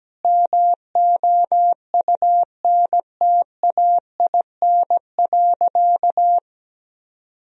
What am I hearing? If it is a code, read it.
Morse code, heard as a keyed tone, about 17 wpm, 700 Hz. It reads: MOUNTAIN.
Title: Sentences from Top 1000 Words - Review Part 1 - 17wpm